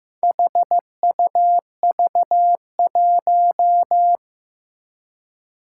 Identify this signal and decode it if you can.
Morse code, heard as a keyed tone, about 15 wpm, 700 Hz. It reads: HUV1
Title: Four Challenging Letter-Number Combinations 15wpm